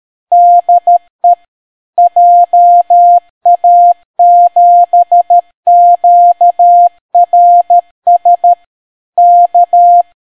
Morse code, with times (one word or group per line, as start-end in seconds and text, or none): 0.3-1.3 DE
2.0-8.5 JA7QRS
9.2-10.0 K